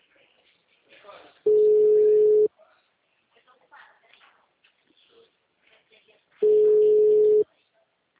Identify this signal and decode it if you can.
Ringback tone (Europe, 425 Hz), with speech.
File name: out- -2002- - - .4